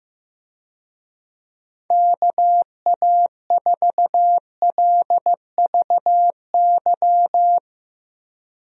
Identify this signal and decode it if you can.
Morse code, heard as a keyed tone, about 15 wpm, 700 Hz. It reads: KA4LVY
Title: USA Call Signs - 6 Characters - Repeated 2x - 15wpm